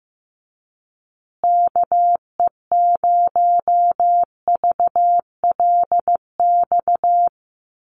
Morse code, heard as a keyed tone, about 15 wpm, 700 Hz. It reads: KE0VLX
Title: USA Call Signs Repeated 3x 15wpm